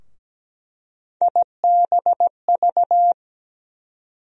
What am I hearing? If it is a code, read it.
Morse code, heard as a keyed tone, about 17 wpm, 700 Hz. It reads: IBV
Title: USA Call Sign Postfixes - 3 Letters 17wpm